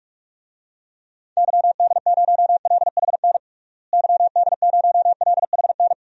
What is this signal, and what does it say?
Morse code, heard as a keyed tone, about 47 wpm, 700 Hz. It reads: YD0LHN YD0LHN
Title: DXCC-100 Call Signs - 6 Characters - Repeated 2x - 45wpm